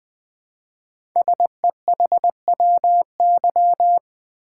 Morse code, heard as a keyed tone, about 20 wpm, 700 Hz. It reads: SEHWY
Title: Code Groups - 5 Letters - 20wpm